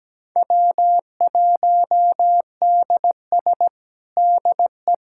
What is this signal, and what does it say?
Morse code, heard as a keyed tone, about 17 wpm, 700 Hz. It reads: W1DS DE
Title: USA Callsign DE Callsign - 4 Characters - 17wpm